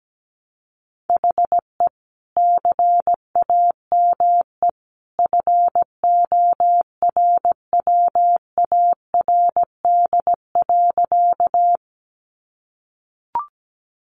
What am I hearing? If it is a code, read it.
Morse code, heard as a keyed tone, about 17 wpm, 700 Hz. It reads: HE CAME FORWARD.
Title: Sentences from Top 1300 Words 17wpm